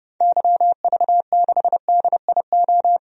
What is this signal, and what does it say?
Morse code, heard as a keyed tone, about 30 wpm, 700 Hz. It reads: YV6DIO
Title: DXCC-100 Call Signs - 6 Characters - 30wpm